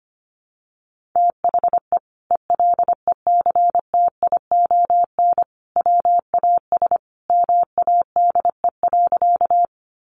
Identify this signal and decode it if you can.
Morse code, heard as a keyed tone, about 25 wpm, 700 Hz. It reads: THE ELECTION WAS MADE.
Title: Sentences from Top 1500 Words 25wpm